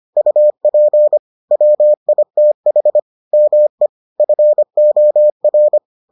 Morse code, heard as a keyed tone, about 25 wpm, 600 Hz. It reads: UP WITH ME FOR